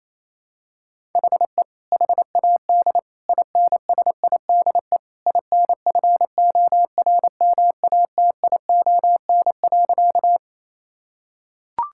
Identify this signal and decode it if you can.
Morse code, heard as a keyed tone, about 28 wpm, 700 Hz. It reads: HE HAD INSIDE INFORMATION.
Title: Sentences from Top 1000 Words - Review Part 6 - 28wpm